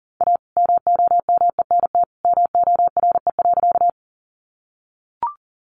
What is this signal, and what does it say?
Morse code, heard as a keyed tone, about 40 wpm, 700 Hz. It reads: A MOMENT MORE.
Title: Sentences from Top 1000 Words 40wpm